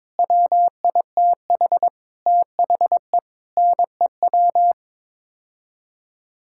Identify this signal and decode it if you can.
Morse code, heard as a keyed tone, about 22 wpm, 700 Hz. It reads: WITH THE NEW
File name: Sets of 9 Words - 22wpm